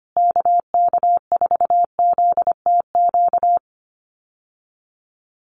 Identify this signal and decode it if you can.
Morse code, heard as a keyed tone, about 25 wpm, 700 Hz. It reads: KK4ZTQ
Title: USA Call Signs Repeated 3x 25wpm